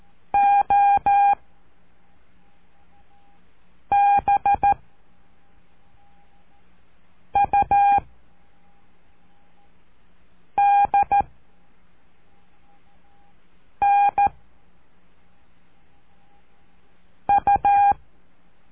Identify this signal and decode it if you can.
Morse code, heard as a keyed tone, about 13 wpm, 810 Hz. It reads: O B U D N U
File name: cursos / CURSO DE TELEGRAFIA / Telegrafia - Trilha 04